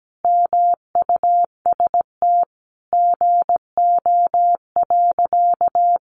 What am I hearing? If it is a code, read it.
Morse code, heard as a keyed tone, about 17 wpm, 700 Hz. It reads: MUST GO.
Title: Sentences from Top 1200 Words 17wpm